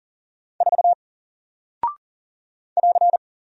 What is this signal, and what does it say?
Morse code, heard as a keyed tone, about 40 wpm, 700 Hz. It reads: 4 <AR>
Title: Single Character (Letter:Number:Punctuation:Prosign) Alt 40wpm